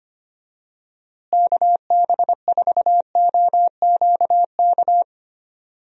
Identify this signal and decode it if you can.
Morse code, heard as a keyed tone, about 25 wpm, 700 Hz. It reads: KB4OQK
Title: USA Call Signs - 6 Characters - Repeated 3x - 25wpm